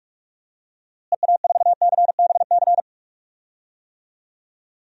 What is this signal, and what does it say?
Morse code, heard as a keyed tone, about 46 wpm, 700 Hz. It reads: EA4CBC
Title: DXCC-100 Call Signs - 6 Characters - Repeated 2x - 45wpm